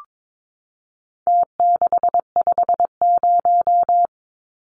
Morse code, heard as a keyed tone, about 22 wpm, 700 Hz. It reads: T650
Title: Numbers - Contest Style - 4-Digit 22wpm